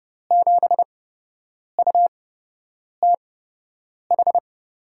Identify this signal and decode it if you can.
Morse code, heard as a keyed tone, about 30 wpm, spaced as for 8 wpm, 700 Hz. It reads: 7UTH
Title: USA Call Signs - 6 Characters - 8x Character Spacing - 30wpm